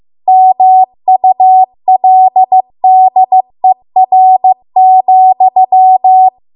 Morse code, heard as a keyed tone, about 15 wpm, 750 Hz. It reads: MULDER,
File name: Archive / 15 WPM / 15WPM